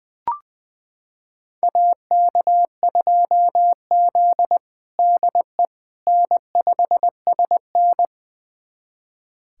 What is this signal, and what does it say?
Morse code, heard as a keyed tone, about 20 wpm, 700 Hz. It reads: AK2Z DE N5SN
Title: USA Callsign DE Callsign - 4 Characters - 20wpm